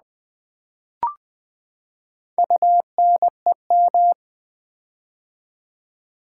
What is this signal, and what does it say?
Morse code, heard as a keyed tone, about 20 wpm, 700 Hz. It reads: UNEM
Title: Build A Word - 4 Letters At A Time - 12-Letter Words - 20wpm